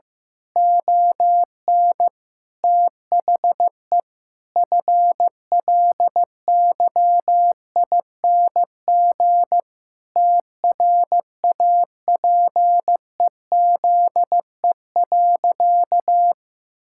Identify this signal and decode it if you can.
Morse code, heard as a keyed tone, about 15 wpm, 700 Hz. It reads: ON THE FLYING TRAPEZE.
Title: Pangram Challenge v2 - 15wpm